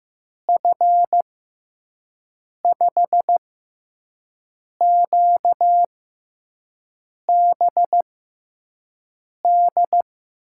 Morse code, heard as a keyed tone, about 15 wpm, spaced as for 5 wpm, 700 Hz. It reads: F5QBD
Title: USA Call Signs - 6 Characters - 6x Character Spacing - 15wpm